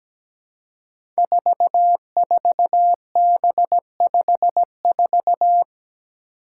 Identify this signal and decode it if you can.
Morse code, heard as a keyed tone, about 17 wpm, 700 Hz. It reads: 44B54
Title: Five Challenging Letter-Number Combinations 17wpm